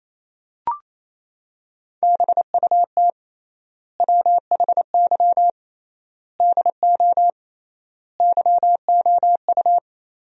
Morse code, heard as a keyed tone, about 27 wpm, 700 Hz. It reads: BUT WHY DO YOU
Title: 3x Word-Spacing - Sentences from Top 200 Words - 28wpm